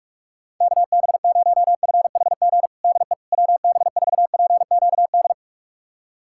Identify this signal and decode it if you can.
Morse code, heard as a keyed tone, about 41 wpm, 700 Hz. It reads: KB0FHG DE WB4PQD
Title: USA Callsign DE Callsign - 6 Characters - 45wpm